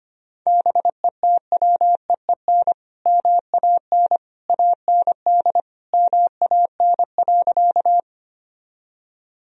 Morse code, heard as a keyed tone, about 25 wpm, 700 Hz. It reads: BETWEEN MAN AND MAN.
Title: Sentences from Top 1000 Words - Review Part 1 - 25wpm